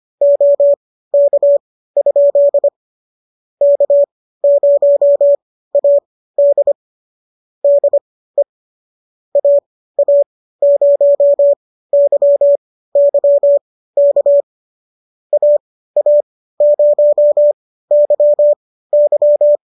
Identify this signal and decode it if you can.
Morse code, heard as a keyed tone, about 25 wpm, spaced as for 15 wpm, 570 Hz. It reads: OK? K0AD DE AA0YY K AA0YY
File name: qso213 15